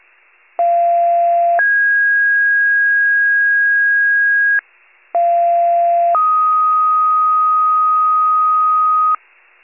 Two-tone page, heard as a one-second tone, then a three-second tone, about 690 then 1650 Hz, about 690 then 1250 Hz, with noise. It is heard from 0.6 to 9.2 s.
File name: wcalert